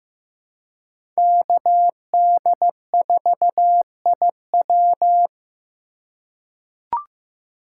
Morse code, heard as a keyed tone, about 15 wpm, 700 Hz. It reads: KD4IW
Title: USA Call Signs - 5 Characters - 4x Character Spacing - 15wpm